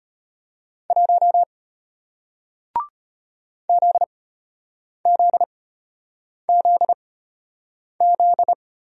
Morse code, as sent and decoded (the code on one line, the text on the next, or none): .---- / --.. --.. --.. --..
1 ZZZZ